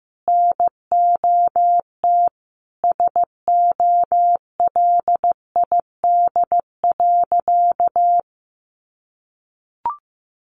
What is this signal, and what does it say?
Morse code, heard as a keyed tone, about 15 wpm, 700 Hz. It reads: NOT SOLID.
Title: Sentences from Top 1500 Words 15wpm